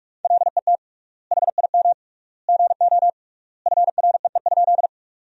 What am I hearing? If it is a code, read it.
Morse code, heard as a keyed tone, about 46 wpm, 700 Hz. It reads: LET HIM GO FREE?
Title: 3x Word-Spacing - Sentences from Top 500 Words - 45wpm